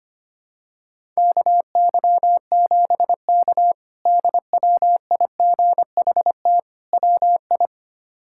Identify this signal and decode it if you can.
Morse code, heard as a keyed tone, about 25 wpm, 700 Hz. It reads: KY7K DWIGHT WI
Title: K1USN SST QSO Practice - 25wpm